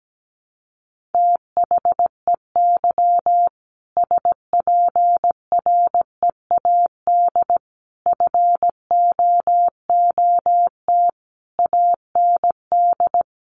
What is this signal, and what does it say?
Morse code, heard as a keyed tone, about 17 wpm, 700 Hz. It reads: THEY SPREAD FOOT AND
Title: Sentences from Top 1500 Words 17wpm